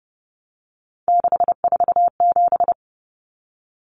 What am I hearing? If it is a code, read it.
Morse code, heard as a keyed tone, about 30 wpm, 700 Hz. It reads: 647
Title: Numbers - 3-Digit 30wpm